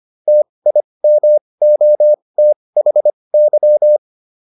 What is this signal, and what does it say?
Morse code, heard as a keyed tone, about 25 wpm, spaced as for 20 wpm, 600 Hz. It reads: TIMOTHY
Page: advanced-practice-files